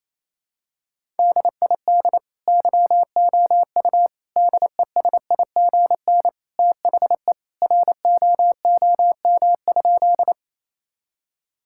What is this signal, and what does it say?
Morse code, heard as a keyed tone, about 27 wpm, 700 Hz. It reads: DID YOU DESIGN THE ROOM?
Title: Sentences from Top 1000 Words - Review Part 5 - 28wpm